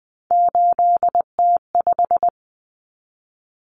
Morse code, heard as a keyed tone, about 20 wpm, 700 Hz. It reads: 8T5
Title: Common Cut Numbers - 3-Digit 20wpm